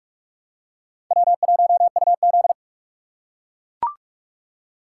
Morse code, heard as a keyed tone, about 43 wpm, 700 Hz. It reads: W1UZ